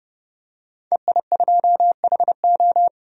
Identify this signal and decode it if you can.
Morse code, heard as a keyed tone, about 31 wpm, 700 Hz. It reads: EI2HO
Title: DXCC-100 Call Signs - 5 Characters - 30wpm